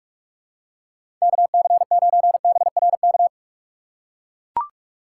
Morse code, heard as a keyed tone, about 43 wpm, 700 Hz. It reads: KC9BRK
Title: USA Call Signs - 4x Character Spacing - 45wpm